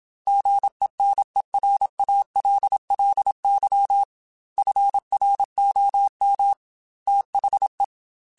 Morse code, heard as a keyed tone, about 27 wpm, 800 Hz. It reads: GENERALLY FROM THE